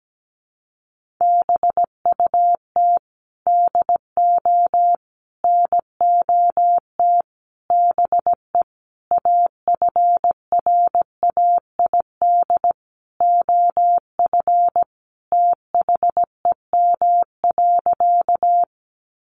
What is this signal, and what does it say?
Morse code, heard as a keyed tone, about 17 wpm, 700 Hz. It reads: BUT DO NOT BE AFRAID OF THEM.
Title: Sentences from Top 1500 Words 17wpm